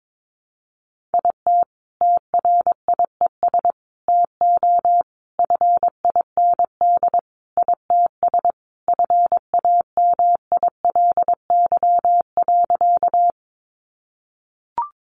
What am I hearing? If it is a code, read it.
Morse code, heard as a keyed tone, about 22 wpm, 700 Hz. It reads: IT TRIES TO FIND ITS FAMILY.
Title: Additional Easy Phrases 22wpm